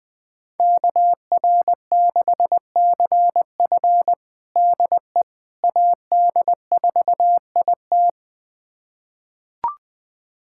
Morse code, heard as a keyed tone, about 20 wpm, 700 Hz. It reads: KR6CF DE AD4IT
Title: USA Callsign DE Callsign - 5 Characters - 20wpm